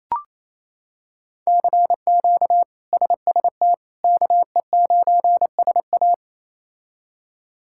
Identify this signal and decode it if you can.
Morse code, heard as a keyed tone, about 28 wpm, 700 Hz. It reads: CQ SST KE9SA
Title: K1USN SST QSO Practice - 28wpm